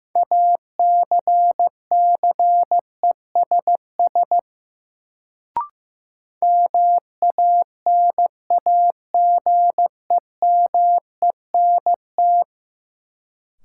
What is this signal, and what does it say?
Morse code, heard as a keyed tone, about 15 wpm, 700 Hz. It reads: ACCESS MANAGEMENT
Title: Sentences from Top 1000 Words - Review Part 4 - New Word Drill - 15wpm